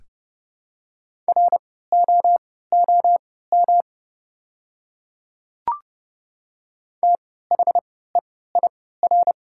Morse code, heard as a keyed tone, about 30 wpm, spaced as for 17 wpm, 700 Hz. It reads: ROOM THEIR
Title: 3x Character Spacing - Top 300 Words - 30wpm